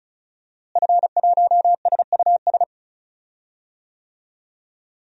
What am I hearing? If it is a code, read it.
Morse code, heard as a keyed tone, about 34 wpm, 700 Hz. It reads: F1SUS
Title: DXCC-100 Call Signs - 5 Characters - Repeated 3x - 35wpm